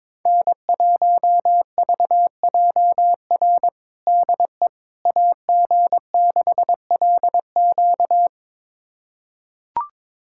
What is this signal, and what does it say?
Morse code, heard as a keyed tone, about 21 wpm, 700 Hz. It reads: N1VJR DE AG6LQ
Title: USA Callsign DE Callsign - 5 Characters - 22wpm